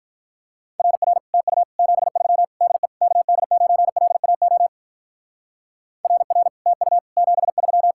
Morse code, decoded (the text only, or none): RR TU 73 DE KD9LAO RR TU 73